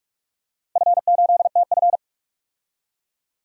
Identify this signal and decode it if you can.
Morse code, heard as a keyed tone, about 42 wpm, 700 Hz. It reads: F8TF
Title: DXCC-100 Call Signs - 4 Characters - 45wpm